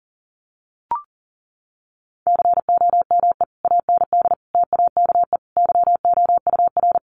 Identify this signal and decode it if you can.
Morse code, heard as a keyed tone, about 40 wpm, 700 Hz. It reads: COME AND TAKE YOUR
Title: Sentences from Top 1500 Words 40wpm